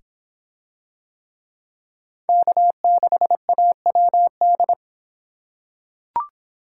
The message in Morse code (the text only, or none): K6AWD